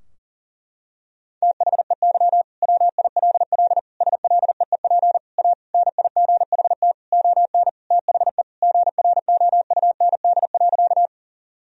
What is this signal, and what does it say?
Morse code, heard as a keyed tone, about 40 wpm, 700 Hz. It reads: THEY WILL SLEEP A NIGHT ON THE GROUND.
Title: Sentences from Top 1000 Words - Review Part 2 - 40wpm